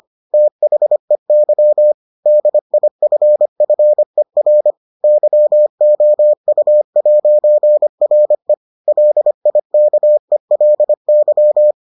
Morse code, heard as a keyed tone, about 25 wpm, 600 Hz. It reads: THEY DIFFER YOU'RE LIKELY